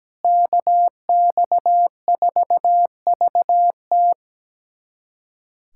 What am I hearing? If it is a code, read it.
Morse code, heard as a keyed tone, about 17 wpm, 700 Hz. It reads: KX4VT
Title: USA Callsign DE Callsign 17wpm